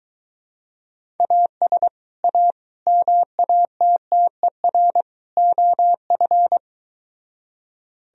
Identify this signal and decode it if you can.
Morse code, heard as a keyed tone, about 22 wpm, 700 Hz. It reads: AS A MATTER OF